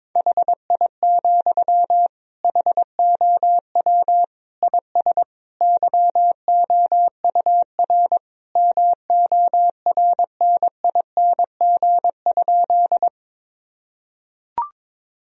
Morse code, heard as a keyed tone, about 22 wpm, 700 Hz. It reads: HI, HOW IS YOUR MORNING?
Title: Sentences from Top 1000 Words - Review Part 3 - 22wpm